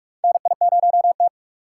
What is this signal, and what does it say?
Morse code, heard as a keyed tone, about 46 wpm, 700 Hz. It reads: NI0T
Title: USA Call Signs - 4x Character Spacing - 45wpm